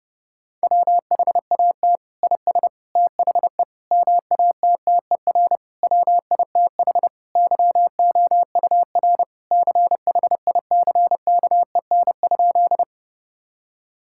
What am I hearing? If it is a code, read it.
Morse code, heard as a keyed tone, about 30 wpm, 700 Hz. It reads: WHAT IS THE MATTER WITH YOUR CHICKEN?
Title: Sentences from Top 1000 Words - Review Part 5 - 30wpm